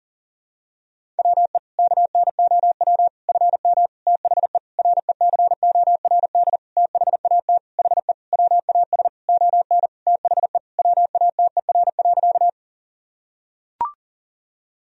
Morse code, heard as a keyed tone, about 40 wpm, 700 Hz. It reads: WE KNOW FM THE RECORD THAT HE WAS ON THE WATER.
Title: Sentences from Top 500 Words and Abbreviations 40wpm